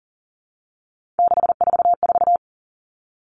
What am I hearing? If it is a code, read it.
Morse code, heard as a keyed tone, about 40 wpm, 700 Hz. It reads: 644
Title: Numbers - 3-Digit 40wpm